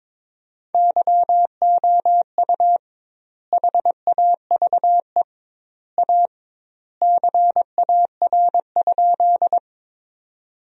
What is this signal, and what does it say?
Morse code, heard as a keyed tone, about 21 wpm, 700 Hz. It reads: YOU HAVE A CAR?
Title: 2x Word-Spacing - Sentences from Top 300 Words - 22wpm